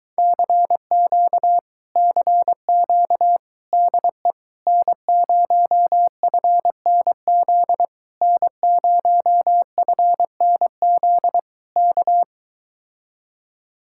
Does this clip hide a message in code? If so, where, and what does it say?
Yes: Morse code from 0.2 to 12.2 s, heard as a keyed tone, about 22 wpm, 700 Hz. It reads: CQ CQ DE N0FNZ N0FNZ K